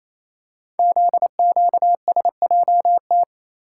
Morse code, heard as a keyed tone, about 27 wpm, 700 Hz. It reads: ZQSJT
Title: Code Groups - 5 Letters - 28wpm